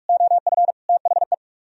Morse code, heard as a keyed tone, about 66 wpm, spaced as for 55 wpm, 700 Hz. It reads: OF THE
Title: Speed-Racing - Sets of 4 Words 45wpm